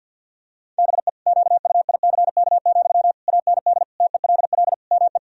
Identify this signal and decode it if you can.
Morse code, heard as a keyed tone, about 50 wpm, 700 Hz. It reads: BE QUICK, AND TELL ME
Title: Sentences from Top 1000 Words - Review Part 3 - 50wpm